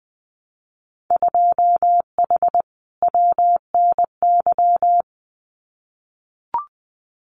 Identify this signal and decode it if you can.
Morse code, heard as a keyed tone, about 20 wpm, 700 Hz. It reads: 2H WNY
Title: Winter Field Day Exchanges Encore 20wpm